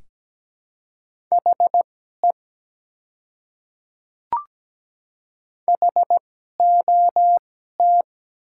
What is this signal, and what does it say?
Morse code, heard as a keyed tone, about 17 wpm, spaced as for 12 wpm, 700 Hz. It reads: HE HOT
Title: 2x Character Spacing - Top 200 Words - 17wpm